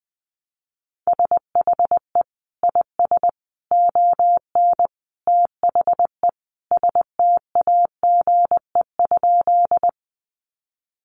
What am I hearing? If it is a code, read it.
Morse code, heard as a keyed tone, about 20 wpm, 700 Hz. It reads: SHE IS ON THE STAGE?
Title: Sentences from Top 1200 Words 20wpm